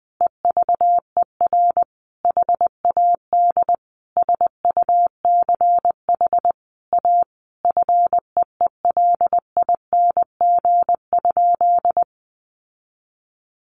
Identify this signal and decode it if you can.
Morse code, heard as a keyed tone, about 20 wpm, 700 Hz. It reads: EVER HAD SUCH A FEELING?
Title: Sentences from Top 1200 Words 20wpm